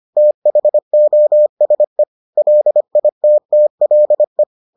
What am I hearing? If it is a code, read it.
Morse code, heard as a keyed tone, about 25 wpm, 600 Hz. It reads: THOSE LITTLE